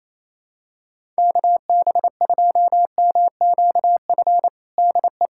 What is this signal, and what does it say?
Morse code, heard as a keyed tone, about 28 wpm, 700 Hz. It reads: KB2MQF DE
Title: USA Callsign DE Callsign 28wpm